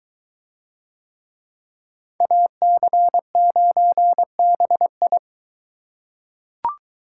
Morse code, heard as a keyed tone, about 22 wpm, 700 Hz. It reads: AC9BI